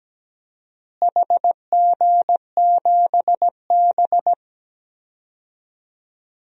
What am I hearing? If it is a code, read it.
Morse code, heard as a keyed tone, about 17 wpm, 700 Hz. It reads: HG7B
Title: DXCC-100 Call Signs - 4 Characters - Repeated 2x - 17wpm